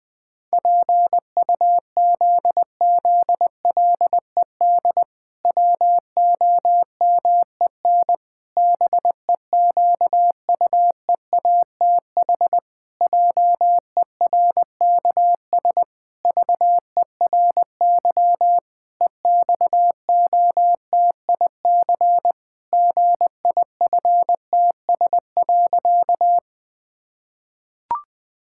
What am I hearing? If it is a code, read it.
Morse code, heard as a keyed tone, about 21 wpm, 700 Hz. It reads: PUZZLED WOMEN BEQUEATH JERKS VERY EXOTIC GIFTS.
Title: Pangram Challenge v2 - 20wpm